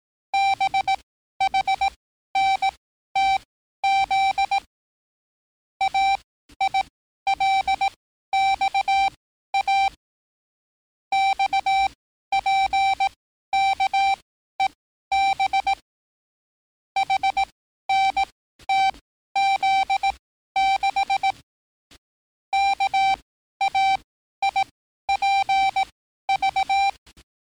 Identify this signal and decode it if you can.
Morse code, heard as a keyed tone, about 18 wpm, spaced as for 12 wpm, 780 Hz. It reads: BHNTZ AILXA XPKEB HNTZ6 KAIPV
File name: training / NAVPERS 11049to11060 / NAVPERS 11054 RA RB